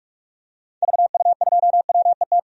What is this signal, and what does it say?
Morse code, heard as a keyed tone, about 46 wpm, 700 Hz. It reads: VU2WET
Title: DXCC-100 Call Signs - 6 Characters - 45wpm